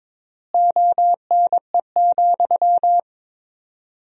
Morse code, heard as a keyed tone, about 22 wpm, 700 Hz. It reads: ONE,
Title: 4x Word-Spacing - Sentences from Top 200 Words - 22wpm